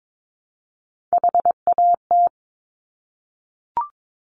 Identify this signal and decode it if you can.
Morse code, heard as a keyed tone, about 22 wpm, 700 Hz. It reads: HAT
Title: Top Three Letter Combinations in Words 22wpm